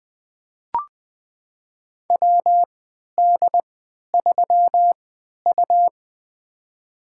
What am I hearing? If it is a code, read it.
Morse code, heard as a keyed tone, about 20 wpm, spaced as for 11 wpm, 700 Hz. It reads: WD3U
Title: USA Call Signs - 4 Characters - 3x Character Spacing - 20wpm